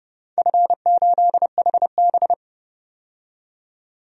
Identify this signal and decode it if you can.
Morse code, heard as a keyed tone, about 30 wpm, 700 Hz. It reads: F8HB
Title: DXCC-100 Call Signs - 4 Characters - 30wpm